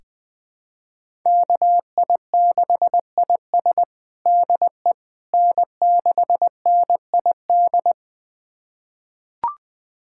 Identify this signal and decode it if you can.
Morse code, heard as a keyed tone, about 20 wpm, 700 Hz. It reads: KI6IS DE N6NID
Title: USA Callsign DE Callsign - 5 Characters - 20wpm